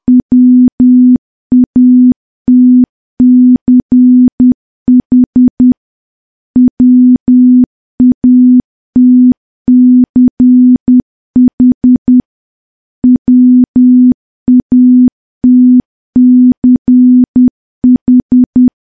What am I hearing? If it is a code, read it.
Morse code, heard as a keyed tone, about 10 wpm, 250 Hz. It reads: WATCH WATCH WATCH
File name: morse / morse-training-10WPM